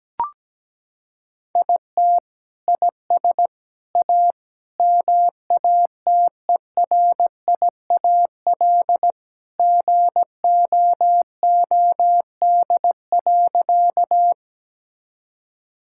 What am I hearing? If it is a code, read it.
Morse code, heard as a keyed tone, about 17 wpm, 700 Hz. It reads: IT IS A MATERIAL GOOD.
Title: Sentences from Top 1000 Words - Review Part 5 - 17wpm